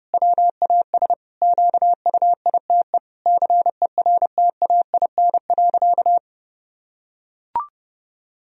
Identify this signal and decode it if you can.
Morse code, heard as a keyed tone, about 30 wpm, 700 Hz. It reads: WAS QUITE CERTAIN.
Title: Sentences from Top 1000 Words - Review Part 2 - 30wpm